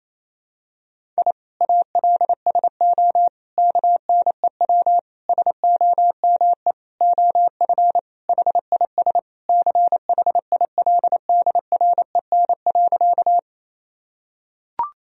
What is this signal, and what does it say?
Morse code, heard as a keyed tone, about 28 wpm, 700 Hz. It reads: I ALSO KNEW SOME OF HIS CHILDREN.
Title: Sentences from Top 1000 Words - Review Part 1 - 28wpm